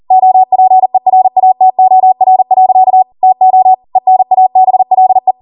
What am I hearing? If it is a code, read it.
Morse code, heard as a keyed tone, about 40 wpm, 750 Hz. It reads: OPERATOR. TO ENABLE
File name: Morse / 40WPM